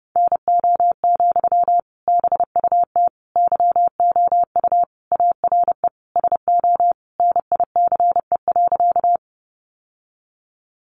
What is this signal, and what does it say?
Morse code, heard as a keyed tone, about 30 wpm, 700 Hz. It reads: NO, BUT YOU ARE SO NICE.
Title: Sentences from Top 1100 Words 30wpm